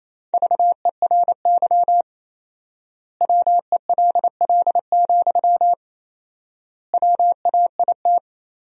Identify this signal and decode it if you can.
Morse code, heard as a keyed tone, about 28 wpm, 700 Hz. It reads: VERY WELL, WAIT
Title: 4x Word-Spacing - Sentences from Top 500 Words - 28wpm